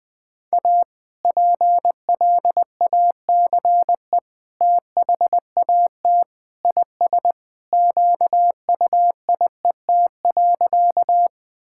Morse code, heard as a keyed tone, about 20 wpm, 700 Hz. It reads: A PLACE THAT IS QUIET.
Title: Sentences with 2-Word Phrases - 20wpm